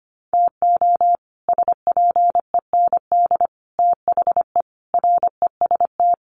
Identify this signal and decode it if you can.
Morse code, heard as a keyed tone, about 25 wpm, 700 Hz. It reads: TO SPEND THE REST
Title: Sets of 5 Words Encore2 25wpm